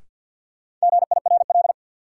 Morse code, heard as a keyed tone, about 49 wpm, 700 Hz. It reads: GIRL
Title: Sentences from Top 400 Words - New Word Drill - 50wpm